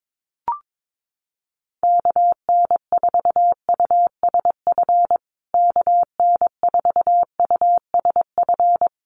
Morse code, heard as a keyed tone, about 22 wpm, 700 Hz. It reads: KN4USF KN4USF
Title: USA Call Signs Repeated 2x 22wpm